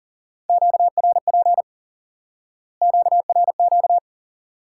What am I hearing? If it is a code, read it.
Morse code, heard as a keyed tone, about 40 wpm, 700 Hz. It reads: QRP QRQ
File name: course / instant-qso-element-course-3 / IQE Course 3 - Lesson 89 - YAGI - Two Elements - 40wpm